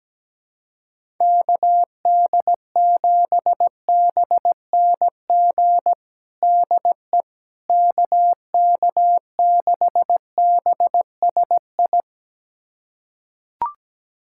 Morse code, heard as a keyed tone, about 17 wpm, 700 Hz. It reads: KD7BNG DE KK6BSI
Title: USA Callsign DE Callsign - 6 Characters - 17wpm